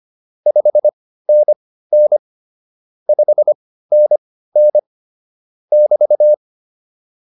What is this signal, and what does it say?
Morse code, heard as a keyed tone, about 25 wpm, spaced as for 15 wpm, 600 Hz. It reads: 5NN 5NN <BT>